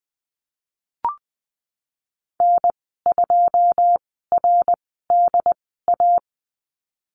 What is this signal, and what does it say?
Morse code, heard as a keyed tone, about 20 wpm, spaced as for 15 wpm, 700 Hz. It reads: N2RDA
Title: 088 Lesson 15g - Number 5 - Call Signs 20wpm